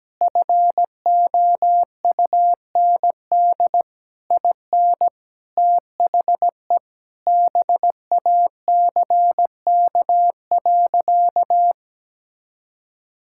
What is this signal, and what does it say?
Morse code, heard as a keyed tone, about 17 wpm, 700 Hz. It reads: FOUND IN THE BACK.
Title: Sentences from Top 1000 Words - Review Part 5 - 17wpm